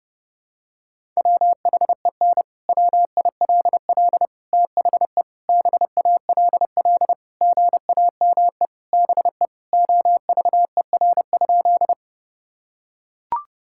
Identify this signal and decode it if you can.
Morse code, heard as a keyed tone, about 30 wpm, 700 Hz. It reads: WHEN WILL THE BALL GAME BE OVER?
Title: Sentences from Top 1000 Words - Review Part 6 - 30wpm